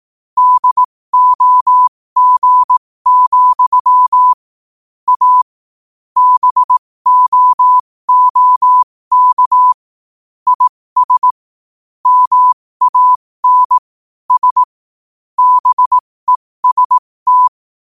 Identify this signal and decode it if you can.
Morse code, heard as a keyed tone, about 18 wpm, spaced as for 15 wpm, 1000 Hz. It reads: DOG, A BOOK IS MAN S BEST
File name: qotdfiles / 15wpm